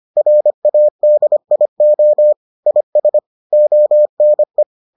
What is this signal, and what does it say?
Morse code, heard as a keyed tone, about 25 wpm, 600 Hz. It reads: RADIO IS ONE